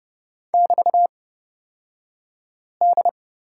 Morse code, heard as a keyed tone, about 31 wpm, spaced as for 5 wpm, 700 Hz. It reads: <BT>D
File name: Lesson 36d - BT - Two Characters 30wpm